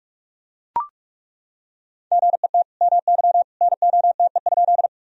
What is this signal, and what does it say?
Morse code, heard as a keyed tone, about 43 wpm, 700 Hz. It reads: GET MY NOTE?
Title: Sentences from Top 1000 Words - Review Part 2 - 45wpm